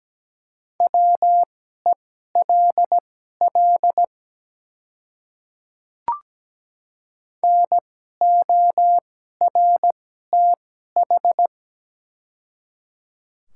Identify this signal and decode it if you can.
Morse code, heard as a keyed tone, about 17 wpm, spaced as for 12 wpm, 700 Hz. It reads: WELL NORTH
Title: 2x Character Spacing - Top 300 Words - 17wpm